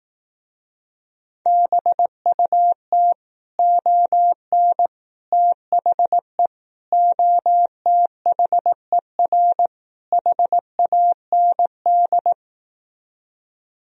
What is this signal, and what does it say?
Morse code, heard as a keyed tone, about 18 wpm, 700 Hz. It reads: BUT ON THE OTHER HAND
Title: Speed-Racing - Sets of 5 Words 15wpm